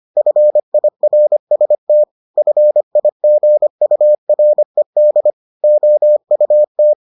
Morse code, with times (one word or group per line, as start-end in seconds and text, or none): 0.2-2.0 FIRST
2.4-5.3 FIGURED
5.6-6.9 OUT